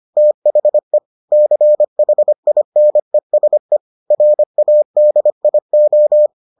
Morse code, heard as a keyed tone, about 25 wpm, 600 Hz. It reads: THE CHINESE RADIO